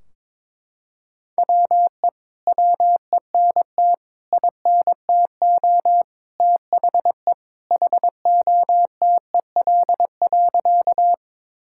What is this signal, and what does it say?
Morse code, heard as a keyed tone, about 21 wpm, 700 Hz. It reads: WE WENT INTO THE HOTEL.
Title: Sentences from Top 1600 Words 22wpm